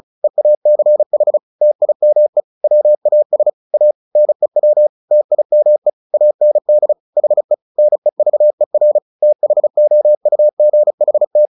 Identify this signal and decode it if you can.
Morse code, heard as a keyed tone, about 34 wpm, 600 Hz. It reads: EACH TIME WAS A NEW TIME AND HE NEVER THOUGHT
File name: the old man and the sea / the old man and the sea 1 35wpm